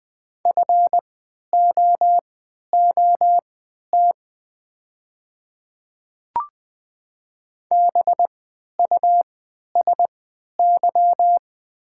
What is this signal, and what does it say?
Morse code, heard as a keyed tone, about 20 wpm, spaced as for 11 wpm, 700 Hz. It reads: FOOT BUSY
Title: 3x Character Spacing - Top 500 Words - 20wpm